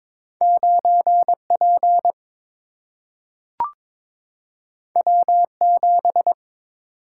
Morse code, heard as a keyed tone, about 22 wpm, 700 Hz. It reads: 9 P W 7
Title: Single-Letter Single-Number 22wpm